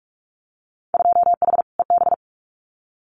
Morse code, heard as a keyed tone, about 46 wpm, 700 Hz. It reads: 2H EB
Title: Winter Field Day Exchanges Encore 45wpm